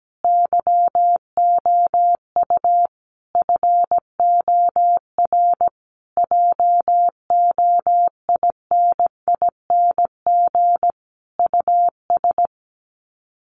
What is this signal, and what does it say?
Morse code, heard as a keyed tone, about 17 wpm, 700 Hz. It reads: YOU FOR JOINING US
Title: Sets of 5 Words 17wpm